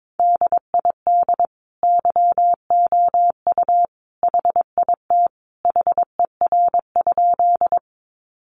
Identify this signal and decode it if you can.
Morse code, heard as a keyed tone, about 22 wpm, 700 Hz. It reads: DID YOU HIT HER?
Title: Sentences from Top 1200 Words 22wpm